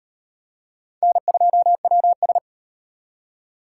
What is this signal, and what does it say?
Morse code, heard as a keyed tone, about 33 wpm, 700 Hz. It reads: N2WS